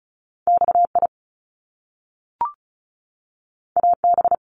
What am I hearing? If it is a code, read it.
Morse code, heard as a keyed tone, about 36 wpm, 700 Hz. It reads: XI AB
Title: Top Two Letter Combinations in Words 35wpm